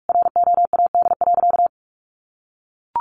Morse code, heard as a keyed tone, about 44 wpm, 700 Hz. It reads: ROAD.
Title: Sentences from Top 1500 Words 45wpm